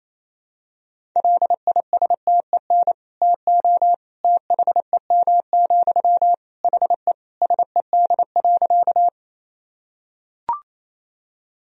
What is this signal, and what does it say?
Morse code, heard as a keyed tone, about 28 wpm, 700 Hz. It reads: LISTEN TO THEM, HE SED.
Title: Sentences from Top 400 Words and Abbreviations 28wpm